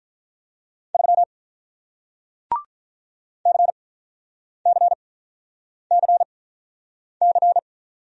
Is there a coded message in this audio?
Yes: Morse code from 0.9 to 7.6 s, heard as a keyed tone, about 48 wpm, spaced as for 9 wpm, 700 Hz. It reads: 3 CCCC